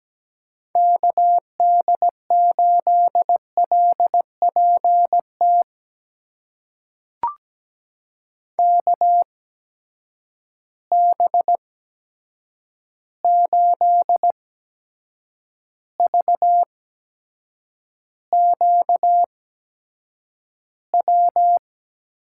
Morse code, heard as a keyed tone, about 17 wpm, spaced as for 7 wpm, 700 Hz. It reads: KD8LPT KB8VQW
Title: USA Call Signs - 8x Character Spacing - 17wpm